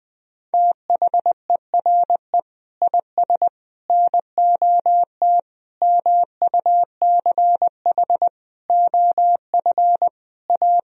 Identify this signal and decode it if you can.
Morse code, heard as a keyed tone, about 21 wpm, 700 Hz. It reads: THERE IS NOT MUCH OF A
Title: Sentences from Top 1000 Words - Review Part 4 - 20wpm